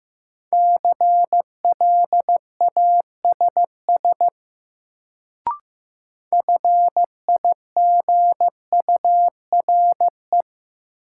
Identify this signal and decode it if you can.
Morse code, heard as a keyed tone, about 15 wpm, 700 Hz. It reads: CLASS FIGURE
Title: Sentences from Top 1000 Words - Review Part 2 - New Word Drill - 15wpm